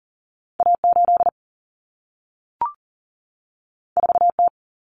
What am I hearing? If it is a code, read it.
Morse code, heard as a keyed tone, about 40 wpm, 700 Hz. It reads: A8 4T
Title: Common Cut Numbers - 2-Digit 40wpm